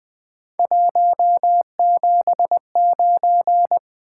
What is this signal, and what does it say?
Morse code, heard as a keyed tone, about 20 wpm, 700 Hz. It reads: 179
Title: Three Letter-Number Combinations - 20wpm